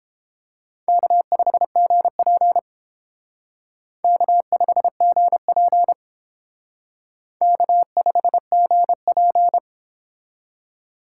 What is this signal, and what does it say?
Morse code, heard as a keyed tone, about 30 wpm, 700 Hz. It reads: K5GP K5GP K5GP